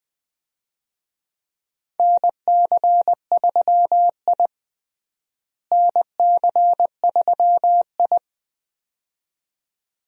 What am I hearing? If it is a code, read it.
Morse code, heard as a keyed tone, about 20 wpm, 700 Hz. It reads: NC3I NC3I
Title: USA Call Signs - 4 Characters - Repeated 2x - 20wpm